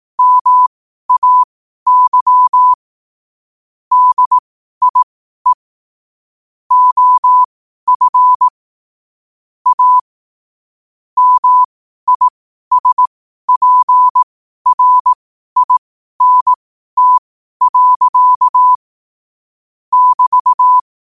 Morse code, heard as a keyed tone, about 18 wpm, spaced as for 12 wpm, 1000 Hz. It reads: MAY DIE OF A MISPRINT. <BT>
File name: qotdfiles / 12wpm